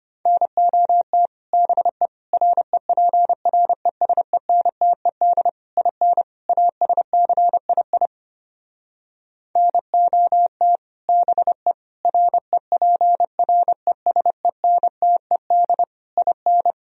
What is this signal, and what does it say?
Morse code, heard as a keyed tone, about 29 wpm, 700 Hz. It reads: NOT BE REPRESENTED IN ASCII NOT BE REPRESENTED IN
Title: Speed-Racing - Sets of 5 Words 25wpm